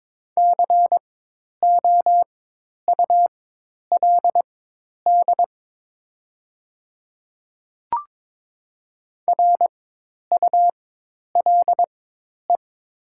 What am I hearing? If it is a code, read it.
Morse code, heard as a keyed tone, about 22 wpm, spaced as for 10 wpm, 700 Hz. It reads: COULD RULE
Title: 4x Character Spacing - Top 500 Words - 22wpm